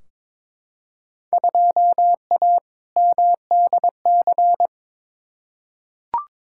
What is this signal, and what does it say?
Morse code, heard as a keyed tone, about 22 wpm, 700 Hz. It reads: 2A MDC
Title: Field Day Exchanges - 22wpm